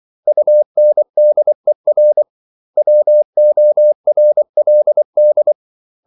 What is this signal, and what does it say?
Morse code, heard as a keyed tone, about 24 wpm, 600 Hz. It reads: UNDER WORLD
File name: CW / Practice / 500MostCommonEnglishWords24